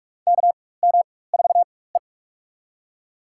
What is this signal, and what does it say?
Morse code, heard as a keyed tone, about 44 wpm, spaced as for 21 wpm, 700 Hz. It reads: KM4E
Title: USA Call Signs - 4 Characters - 4x Character Spacing - 45wpm